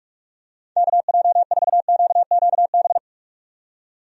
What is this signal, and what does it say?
Morse code, heard as a keyed tone, about 44 wpm, 700 Hz. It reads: KJ4QQB
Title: USA Call Signs - 6 Characters - 45wpm